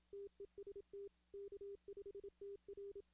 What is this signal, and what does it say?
Morse code, heard as a keyed tone, about 27 wpm, 400 Hz. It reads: TEST K5TR
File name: 2016 arrl dx cw / 2016-ARRL-DX-CW-K5TR- - 0-41